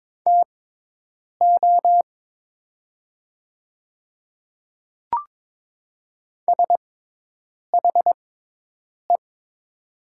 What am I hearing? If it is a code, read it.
Morse code, heard as a keyed tone, about 22 wpm, spaced as for 8 wpm, 700 Hz. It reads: TO SHE
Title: 6x Character Spacing - Top 100 Words - 22wpm